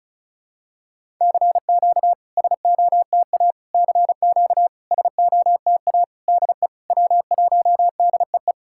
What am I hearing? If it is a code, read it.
Morse code, heard as a keyed tone, about 36 wpm, 690 Hz. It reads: CQ SOTA CQ SOTA DE W1DEE
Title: SOTA QSO Practice 35wpm